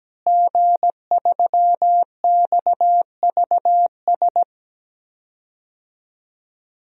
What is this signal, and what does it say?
Morse code, heard as a keyed tone, about 17 wpm, 700 Hz. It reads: G3XVS
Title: DXCC-100 Call Signs - 5 Characters - Repeated 2x - 17wpm